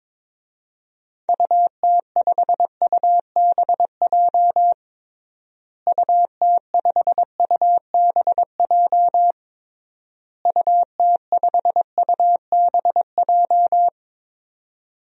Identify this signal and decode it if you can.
Morse code, heard as a keyed tone, about 22 wpm, 700 Hz. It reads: UT5UBJ UT5UBJ UT5UBJ
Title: DXCC-100 Call Signs - 6 Characters - Repeated 3x - 22wpm